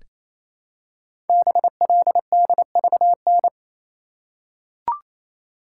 Morse code, heard as a keyed tone, about 29 wpm, 700 Hz. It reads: BLDVN
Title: Code Groups - 5 Letter:Numbers - 28wpm